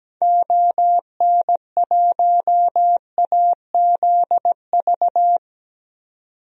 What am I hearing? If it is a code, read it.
Morse code, heard as a keyed tone, about 17 wpm, 700 Hz. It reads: ON1AZV
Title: DXCC-100 Call Signs - 6 Characters - 17wpm